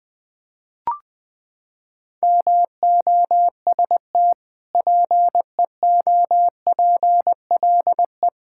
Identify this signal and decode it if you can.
Morse code, heard as a keyed tone, about 20 wpm, 700 Hz. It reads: MOST PEOPLE
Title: Sentences from Top 1000 Words - Review Part 5 - 20wpm